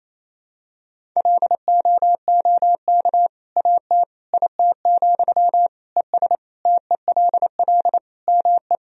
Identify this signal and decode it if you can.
Morse code, heard as a keyed tone, about 28 wpm, 700 Hz. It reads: LOOK AT IT, ES TELL ME
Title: Sentences from Top 200 Words and Abbreviations 28wpm